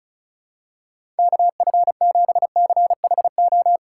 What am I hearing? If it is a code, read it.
Morse code, heard as a keyed tone, about 36 wpm, 700 Hz. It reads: KF7CHO
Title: USA Call Signs - 6 Characters - 4x Character Spacing - 35wpm